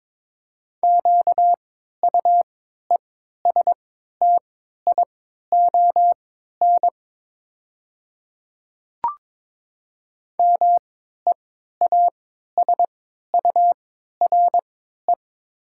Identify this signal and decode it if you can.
Morse code, heard as a keyed tone, about 22 wpm, spaced as for 13 wpm, 700 Hz. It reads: QUESTION MEASURE
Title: 3x Character Spacing - Top 500 Words - 22wpm